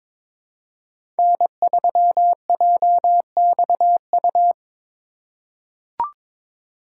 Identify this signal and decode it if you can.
Morse code, heard as a keyed tone, about 22 wpm, 700 Hz. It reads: N3JXU
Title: USA Call Signs - 3x Character Spacing - 22wpm